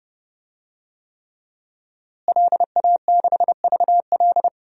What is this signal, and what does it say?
Morse code, heard as a keyed tone, about 30 wpm, 700 Hz. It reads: LA6VL
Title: DXCC-100 Call Signs - 5 Characters - Repeated 3x - 30wpm